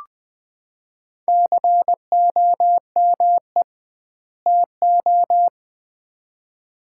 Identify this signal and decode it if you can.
Morse code, heard as a keyed tone, about 20 wpm, 700 Hz. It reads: COME TO
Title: 2x Word-Spacing - Sets of 2 Words - 20wpm